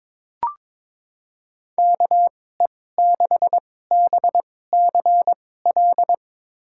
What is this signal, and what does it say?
Morse code, heard as a keyed tone, about 22 wpm, spaced as for 16 wpm, 700 Hz. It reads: KE6BCL
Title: USA Call Signs - 6 Characters - 2x Character Spacing - 22wpm